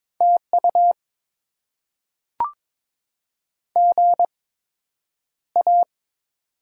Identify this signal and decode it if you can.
Morse code, heard as a keyed tone, about 22 wpm, spaced as for 9 wpm, 700 Hz. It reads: TU GA
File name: Top 20 QSO Elements - 8x Character Spacing - 22wpm